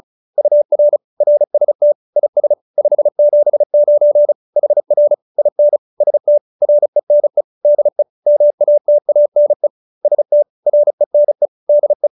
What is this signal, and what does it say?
Morse code, heard as a keyed tone, about 34 wpm, 590 Hz. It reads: UR RST IS 579 HR IN ST RENE DE MATANE ST RENE DE